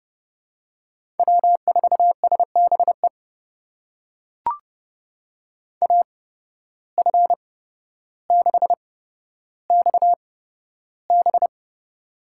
Morse code, heard as a keyed tone, about 30 wpm, spaced as for 13 wpm, 700 Hz. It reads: W4SBE AF6XB
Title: USA Call Signs - 5 Characters - 8x Character Spacing - 30wpm